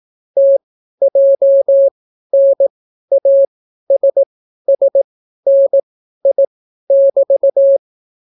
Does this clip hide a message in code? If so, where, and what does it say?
Yes: Morse code from 0.4 to 7.8 s, heard as a keyed tone, about 18 wpm, spaced as for 12 wpm, 550 Hz. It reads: TJNASSNI<BT>